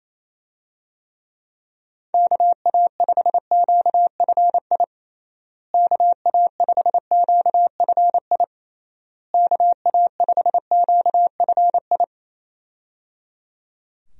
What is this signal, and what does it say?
Morse code, heard as a keyed tone, about 28 wpm, 700 Hz. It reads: KA5QFI KA5QFI KA5QFI
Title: USA Call Signs - 6 Characters - Repeated 3x - 28wpm